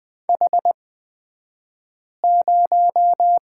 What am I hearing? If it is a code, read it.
Morse code, heard as a keyed tone, about 20 wpm, spaced as for 5 wpm, 700 Hz. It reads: H0
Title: 228 Lesson 37d - Letter x - Two Characters 20wpm